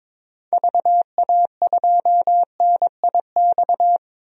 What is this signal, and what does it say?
Morse code, heard as a keyed tone, about 22 wpm, 700 Hz. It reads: VA2NIX
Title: DXCC-100 Call Signs - 6 Characters - 22wpm